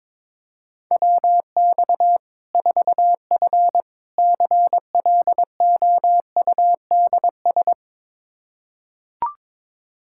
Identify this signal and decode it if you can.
Morse code, heard as a keyed tone, about 22 wpm, 700 Hz. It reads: WX 4F CLOUDS
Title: Practice QSO - Exchange 2 - Info - 22wpm